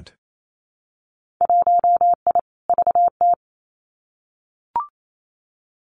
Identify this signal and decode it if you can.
Morse code, heard as a keyed tone, about 28 wpm, 700 Hz. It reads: 1I VT
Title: Winter Field Day Exchanges Encore 28wpm